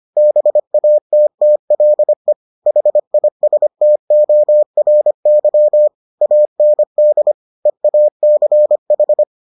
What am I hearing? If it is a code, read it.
Morse code, heard as a keyed tone, about 25 wpm, 600 Hz. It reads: BATTLE HISTORY AND EACH